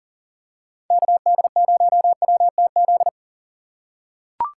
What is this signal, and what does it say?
Morse code, heard as a keyed tone, about 40 wpm, 700 Hz. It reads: KD0WTZ
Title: USA Call Signs - 6 Characters - 4x Character Spacing - 40wpm